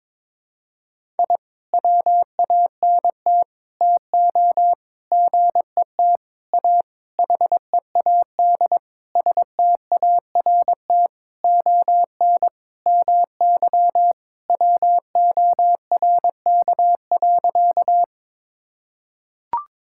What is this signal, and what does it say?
Morse code, heard as a keyed tone, about 22 wpm, 700 Hz. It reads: I WANT TO GET A HEAD START ON MY WORK.
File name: Sentences with 7-Word Phrases - 22wpm